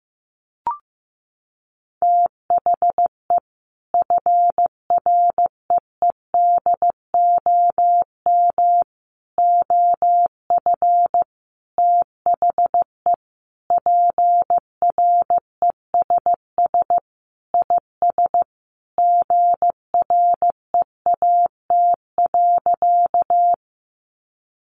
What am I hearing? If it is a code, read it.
Morse code, heard as a keyed tone, about 15 wpm, 700 Hz. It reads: THE FREEDOM OF THE PRESS IS GREAT.
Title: Sentences from Top 800 Words 15wpm